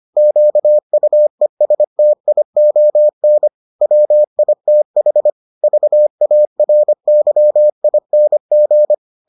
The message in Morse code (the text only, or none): QUESTION WITH VARYING